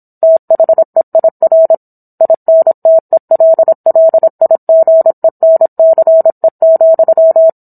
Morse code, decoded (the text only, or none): THEIR INTELLIGENCE,